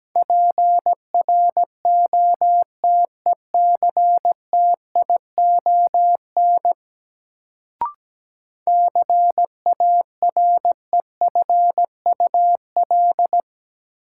Morse code, heard as a keyed tone, about 17 wpm, 700 Hz. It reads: PROTECTION CAREFUL
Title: Sentences from Top 1600 Words - New Word Drill Alt - 17wpm